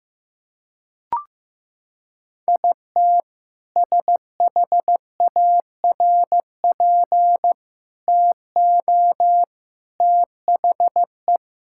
Morse code, heard as a keyed tone, about 15 wpm, 700 Hz. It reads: IT SHARP TO THE
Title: Sentences from Top 1600 Words 15wpm